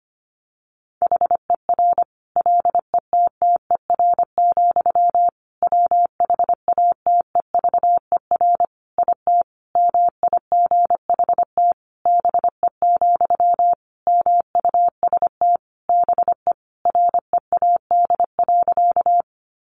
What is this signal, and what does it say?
Morse code, heard as a keyed tone, about 25 wpm, 700 Hz. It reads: HER LETTER, WHATEVER IT MIGHT BE, MUST BE READ.
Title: Sentences from Top 1100 Words 25wpm